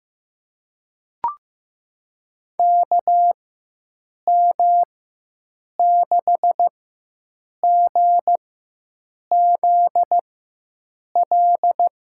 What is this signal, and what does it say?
Morse code, heard as a keyed tone, about 15 wpm, spaced as for 7 wpm, 700 Hz. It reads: KM6GZL
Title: USA Call Signs - 6 Characters - 4x Character Spacing - 15wpm